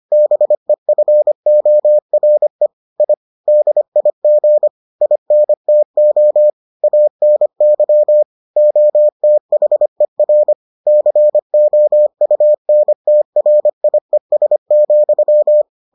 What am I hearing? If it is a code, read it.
Morse code, heard as a keyed tone, about 25 wpm, 600 Hz. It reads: BEFORE I DIG INTO ANY OTHER COUNTRIES,